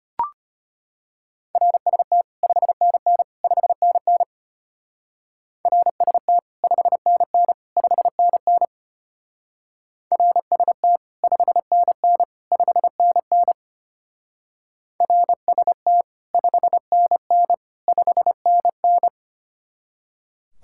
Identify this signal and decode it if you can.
Morse code, heard as a keyed tone, about 29 wpm, 700 Hz. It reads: RST 5NN 5NN RST 5NN 5NN RST 5NN 5NN RST 5NN 5NN
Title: Speed-Racing Common QSO Phrases - 25wpm